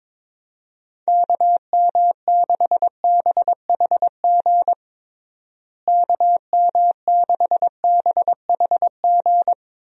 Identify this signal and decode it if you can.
Morse code, heard as a keyed tone, about 22 wpm, 700 Hz. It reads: KM6BHG KM6BHG
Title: USA Call Signs - 6 Characters - Repeated 3x - 22wpm